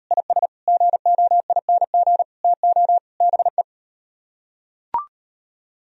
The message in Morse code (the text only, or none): IS GOING TO BE